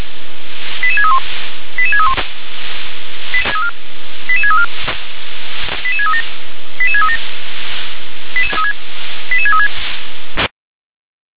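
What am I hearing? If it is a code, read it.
Five-tone selective call (ZVEI), heard as quick runs of five tones, tones of about 80 ms. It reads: 88631 88631 88634 88637 88637 88636
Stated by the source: ZVEI five-tone selective call received by SDR, address 88730.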